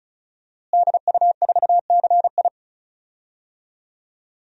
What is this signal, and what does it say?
Morse code, heard as a keyed tone, about 34 wpm, 700 Hz. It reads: DU4CI
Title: DXCC-100 Call Signs - 5 Characters - Repeated 2x - 35wpm